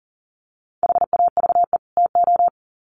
Morse code, heard as a keyed tone, about 40 wpm, 700 Hz. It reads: HAVE TO
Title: Sets of 2 Words 40wpm